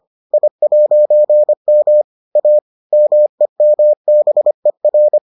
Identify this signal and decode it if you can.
Morse code, heard as a keyed tone, about 25 wpm, 600 Hz. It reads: I'M A MEMBER